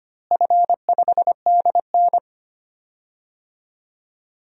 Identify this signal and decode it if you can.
Morse code, heard as a keyed tone, about 25 wpm, 700 Hz. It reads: F5DN
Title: DXCC-100 Call Signs - 4 Characters - Repeated 2x - 25wpm